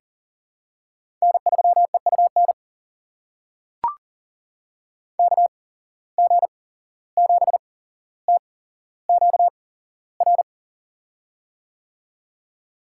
Morse code, heard as a keyed tone, about 40 wpm, spaced as for 16 wpm, 700 Hz. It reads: N3EUN KG7TQR
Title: USA Call Signs - 8x Character Spacing - 40wpm